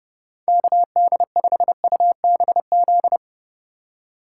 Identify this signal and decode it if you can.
Morse code, heard as a keyed tone, about 32 wpm, 700 Hz. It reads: KD5UBZ
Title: USA Call Signs - 6 Characters - 4x Character Spacing - 30wpm